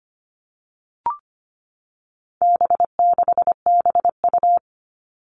Morse code, heard as a keyed tone, about 25 wpm, 700 Hz. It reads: B6BU
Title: All Cut Numbers - 4-Digit 25wpm